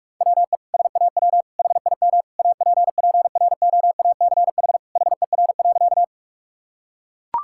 Morse code, heard as a keyed tone, about 44 wpm, 700 Hz. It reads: WE SAW HIM APPROACH HER.
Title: Sentences from Top 1000 Words - Review Part 6 - 45wpm